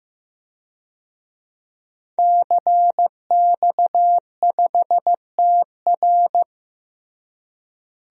Morse code, heard as a keyed tone, about 15 wpm, 700 Hz. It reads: CX5TR
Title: DXCC-100 Call Signs - 5 Characters - Repeated 3x - 15wpm